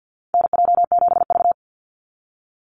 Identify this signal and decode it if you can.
Morse code, heard as a keyed tone, about 48 wpm, 700 Hz. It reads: NJ7V
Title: NA SOTA Chaser Callsigns 50wpm